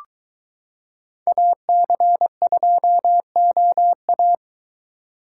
Morse code, heard as a keyed tone, about 23 wpm, 700 Hz. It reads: AC2OA